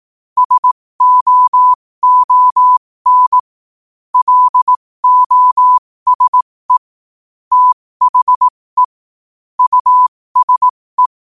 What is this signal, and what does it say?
Morse code, heard as a keyed tone, about 18 wpm, spaced as for 15 wpm, 1000 Hz. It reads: SOON LOSE THE USE
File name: qotdfiles / 15wpm